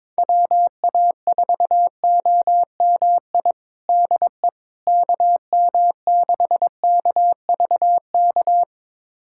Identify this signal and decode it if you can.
Morse code, heard as a keyed tone, about 22 wpm, 700 Hz. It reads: WA4OMI DE KM6KVK
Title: USA Callsign DE Callsign - 6 Characters - 22wpm